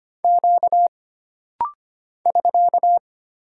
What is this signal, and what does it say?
Morse code, heard as a keyed tone, about 25 wpm, 700 Hz. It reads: Q <SK>
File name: Single Character (Letter:Number:Punctuation:Prosign) - Rapid-Fire - 25wpm